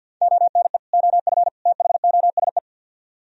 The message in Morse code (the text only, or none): ONE OF THOSE